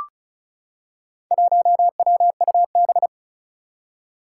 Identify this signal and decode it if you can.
Morse code, heard as a keyed tone, about 36 wpm, 700 Hz. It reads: 1WUB
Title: Four Challenging Letter-Number Combinations 35wpm